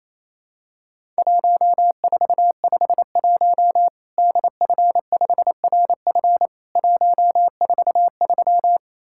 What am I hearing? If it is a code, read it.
Morse code, heard as a keyed tone, about 28 wpm, 700 Hz. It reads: 1451 DF5RF 143
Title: Worked All Europe DX Contest - QTC Exchange - 28wpm